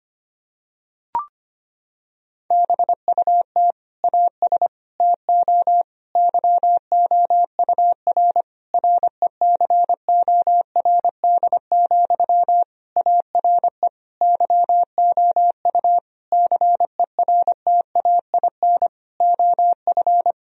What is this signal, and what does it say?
Morse code, heard as a keyed tone, about 25 wpm, 700 Hz. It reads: BUT AS TO YOUR RECORD, ARE YOU CERTAIN OF
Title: Sentences from Top 1000 Words - Review Part 3 - 25wpm